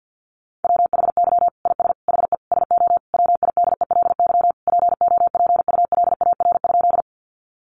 Morse code, heard as a keyed tone, about 50 wpm, 700 Hz. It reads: WHY IS HE SO WIDELY POPULAR?
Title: Sentences from Top 1200 Words 50wpm